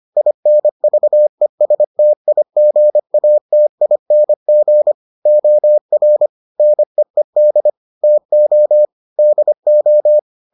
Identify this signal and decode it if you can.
Morse code, heard as a keyed tone, about 25 wpm, 600 Hz. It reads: INVESTIGATING OR NEED TO DO